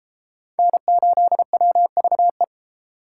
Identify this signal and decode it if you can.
Morse code, heard as a keyed tone, about 30 wpm, 700 Hz. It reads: N8WVE